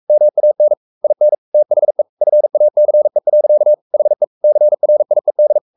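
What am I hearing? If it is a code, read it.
Morse code, heard as a keyed tone, about 44 wpm, 600 Hz. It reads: MAN IN THE FACE. HE CRIED
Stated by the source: Morse code at 43 wpm sending Ken Follett's novel Never in English, from CW med Gnister.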